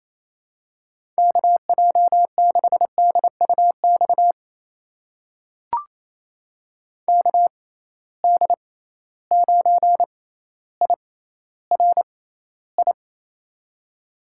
Morse code, heard as a keyed tone, about 27 wpm, spaced as for 14 wpm, 700 Hz. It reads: KJ6DUX KD9IRI
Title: USA Call Signs - 6 Characters - 6x Character Spacing - 28wpm